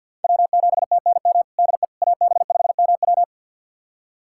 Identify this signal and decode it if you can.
Morse code, heard as a keyed tone, about 51 wpm, 700 Hz. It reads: W7TNM DE AB5MW
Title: USA Callsign DE Callsign - 5 Characters - 50wpm